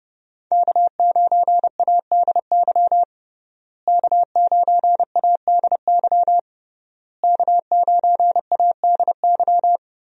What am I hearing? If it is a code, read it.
Morse code, heard as a keyed tone, about 30 wpm, 700 Hz. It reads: K9ADY K9ADY K9ADY
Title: USA Call Signs - 5 Characters - Repeated 3x - 30wpm